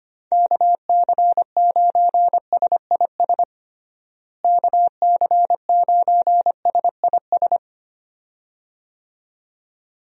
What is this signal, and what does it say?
Morse code, heard as a keyed tone, about 25 wpm, 700 Hz. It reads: KC9SIS KC9SIS
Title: USA Call Signs - 6 Characters - Repeated 2x - 25wpm